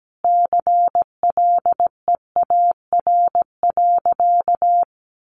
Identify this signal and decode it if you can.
Morse code, heard as a keyed tone, about 17 wpm, 700 Hz. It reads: CLEAR.
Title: Sentences from Top 1200 Words 17wpm